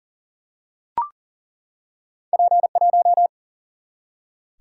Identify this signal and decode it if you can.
Morse code, heard as a keyed tone, about 40 wpm, 700 Hz. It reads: P 1
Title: Single-Letter Single-Number 40wpm